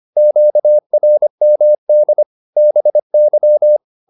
Morse code, heard as a keyed tone, about 25 wpm, 600 Hz. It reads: QRMD BY